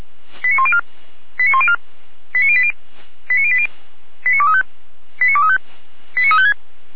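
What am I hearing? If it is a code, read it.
Five-tone selective call (ZVEI), heard as quick runs of five tones, tones of about 70 ms. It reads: 89104 89104 89080 89080 89235 89235 89356
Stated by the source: ZVEI five-tone call 89304 received by SDR.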